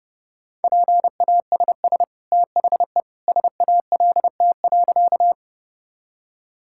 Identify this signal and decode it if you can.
Morse code, heard as a keyed tone, about 32 wpm, 700 Hz. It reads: PASS THE SALT.
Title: Sentences from Top 1000 Words - Review Part 5 - 30wpm